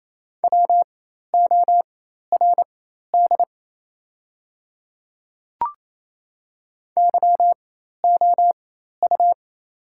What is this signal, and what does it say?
Morse code, heard as a keyed tone, about 28 wpm, spaced as for 13 wpm, 700 Hz. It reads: WORD YOU
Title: 4x Character Spacing - Top 100 Words - 28wpm